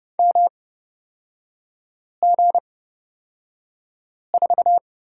Morse code, heard as a keyed tone, about 30 wpm, spaced as for 5 wpm, 700 Hz. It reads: MG4
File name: Lesson 37e - Letter x - Three Characters 30wpm